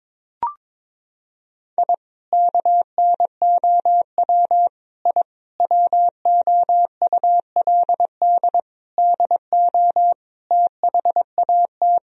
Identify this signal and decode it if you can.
Morse code, heard as a keyed tone, about 21 wpm, 700 Hz. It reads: I KNOW I WOULD DO THAT
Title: Sentences from Top 1000 Words - Review Part 3 - 22wpm